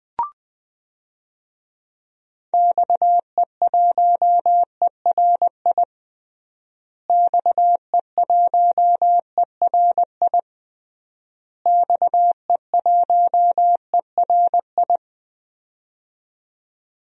Morse code, heard as a keyed tone, about 20 wpm, 700 Hz. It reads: XE1ERI XE1ERI XE1ERI
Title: DXCC-100 Call Signs - 6 Characters - Repeated 3x - 20wpm